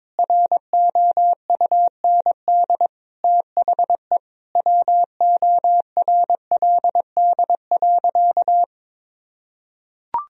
Morse code, heard as a keyed tone, about 22 wpm, 700 Hz. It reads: ROUND THE WORLD.
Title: Sentences from Top 1000 Words - Review Part 4 - 22wpm